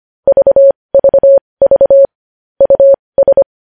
Morse code, heard as a keyed tone, about 25 wpm, spaced as for 20 wpm, 550 Hz. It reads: VVV US
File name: podcast / News headlines-1x-25x20